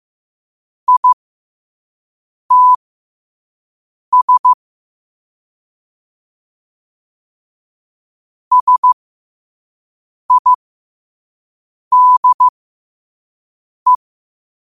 Morse code, heard as a keyed tone, about 15 wpm, spaced as for 5 wpm, 1000 Hz. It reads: ITS SIDE
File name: qotdfiles / 5wpm